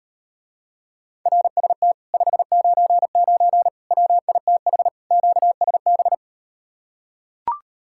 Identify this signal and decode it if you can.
Morse code, heard as a keyed tone, about 39 wpm, 700 Hz. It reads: RST 599 WITH QSB